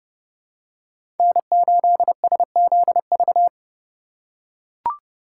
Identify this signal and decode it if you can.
Morse code, heard as a keyed tone, about 32 wpm, 700 Hz. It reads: N8SZV
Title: USA Call Signs - 5 Characters - 2x Character Spacing - 30wpm